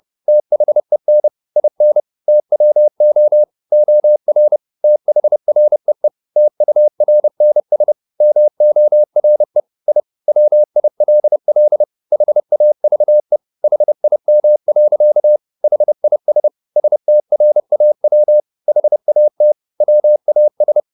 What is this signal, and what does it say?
Morse code, heard as a keyed tone, about 30 wpm, 600 Hz. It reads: THEN IN TWO OR THREE TURNS MORE I WILL HAVE HIM. HIS STRAW HAT WAS